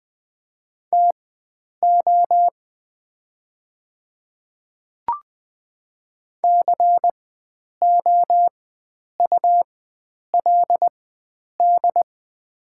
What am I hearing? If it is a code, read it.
Morse code, heard as a keyed tone, about 20 wpm, spaced as for 9 wpm, 700 Hz. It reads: TO COULD
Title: 4x Character Spacing - Top 100 Words - 20wpm